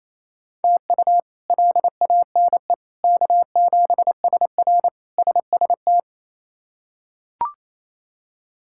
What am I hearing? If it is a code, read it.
Morse code, heard as a keyed tone, about 28 wpm, 700 Hz. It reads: TU LANE K7SR SST
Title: K1USN SST QSO Practice - 28wpm